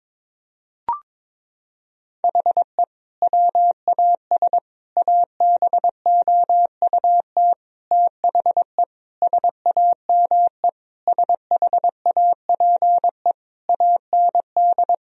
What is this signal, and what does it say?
Morse code, heard as a keyed tone, about 21 wpm, 700 Hz. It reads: HE WAS ABOUT THE SAME SHAPE AND
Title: Sentences from Top 1000 Words - Review Part 3 - 22wpm